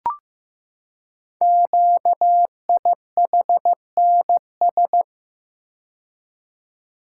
Morse code, heard as a keyed tone, about 15 wpm, 700 Hz. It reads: QIHNS
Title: Code Groups - 5 Letters - 15wpm